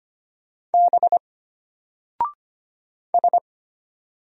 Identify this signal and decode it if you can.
Morse code, heard as a keyed tone, about 25 wpm, 700 Hz. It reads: B S
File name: Single-Letter-Number-25wpm